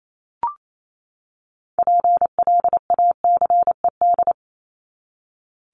Morse code, heard as a keyed tone, about 28 wpm, 700 Hz. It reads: PLACED
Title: ED Suffix - Top 100 Words with Base Words 28wpm